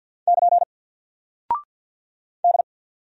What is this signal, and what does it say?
Morse code, heard as a keyed tone, about 50 wpm, 700 Hz. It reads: <KN> D
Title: Single Character (Letter:Number:Punctuation:Prosign) 50wpm